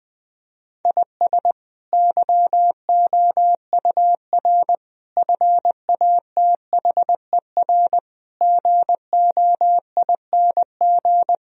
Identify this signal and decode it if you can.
Morse code, heard as a keyed tone, about 20 wpm, 700 Hz. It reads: IS YOUR FATHER GOING
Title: Sentences from Top 1000 Words - Review Part 2 - 20wpm